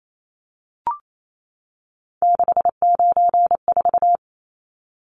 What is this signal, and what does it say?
Morse code, heard as a keyed tone, about 28 wpm, 700 Hz. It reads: 694
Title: Numbers - 3-Digit 28wpm